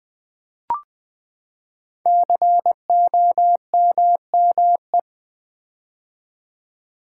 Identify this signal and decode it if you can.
Morse code, heard as a keyed tone, about 20 wpm, 700 Hz. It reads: COMME
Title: Build A Word - 5 Letters At A Time - 10-Letter Words - 20wpm